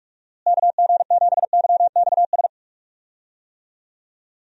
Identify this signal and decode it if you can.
Morse code, heard as a keyed tone, about 47 wpm, 700 Hz. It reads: KG7YXS
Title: USA Call Signs - 6 Characters - Repeated 3x - 45wpm